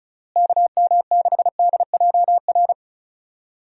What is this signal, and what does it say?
Morse code, heard as a keyed tone, about 36 wpm, 700 Hz. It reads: KM6DJR
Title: USA Call Signs - 6 Characters - Repeated 3x - 35wpm